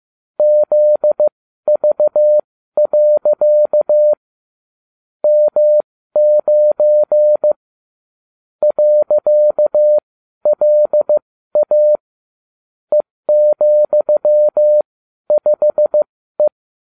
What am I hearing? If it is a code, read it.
Morse code, heard as a keyed tone, about 15 wpm, spaced as for 12 wpm, 600 Hz. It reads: ZV. M9 .LA E,5E